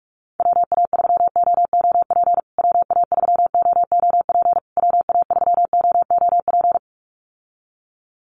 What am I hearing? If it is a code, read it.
Morse code, heard as a keyed tone, about 44 wpm, 700 Hz. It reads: WA3OOP WA3OOP WA3OOP
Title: USA Call Signs Repeated 3x 45wpm